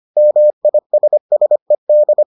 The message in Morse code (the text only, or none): MISSED